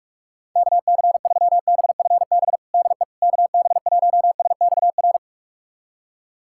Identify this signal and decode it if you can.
Morse code, heard as a keyed tone, about 46 wpm, 690 Hz. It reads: KC3BFB DE KB1SXR
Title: USA Callsign DE Callsign - 6 Characters - 45wpm